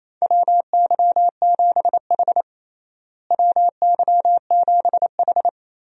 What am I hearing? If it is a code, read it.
Morse code, heard as a keyed tone, about 27 wpm, 700 Hz. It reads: WY7H WY7H
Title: USA Call Signs - 4 Characters - Repeated 3x - 28wpm